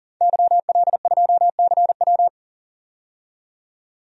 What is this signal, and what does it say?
Morse code, heard as a keyed tone, about 39 wpm, 700 Hz. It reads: YL2CW
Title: DXCC-100 Call Signs - 5 Characters - 40wpm